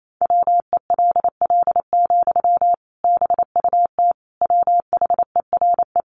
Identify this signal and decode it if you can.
Morse code, heard as a keyed tone, about 28 wpm, 700 Hz. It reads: WELL, BUT WHERE
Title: Sentences from Top 500 Words 28wpm